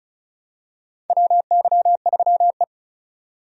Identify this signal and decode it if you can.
Morse code, heard as a keyed tone, about 34 wpm, 700 Hz. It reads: WY3E
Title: USA Call Signs - 4 Characters - 35wpm